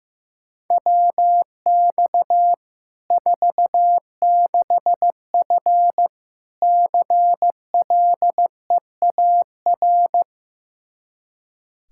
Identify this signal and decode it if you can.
Morse code, heard as a keyed tone, about 15 wpm, 700 Hz. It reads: WX 46F CLEAR
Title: Practice QSO - Exchange 2 - Info - 15wpm